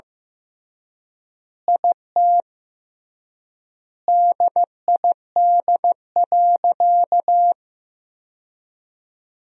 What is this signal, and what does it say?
Morse code, heard as a keyed tone, about 16 wpm, 700 Hz. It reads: IT DID.
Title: 3x Word-Spacing - Sentences from Top 500 Words - 15wpm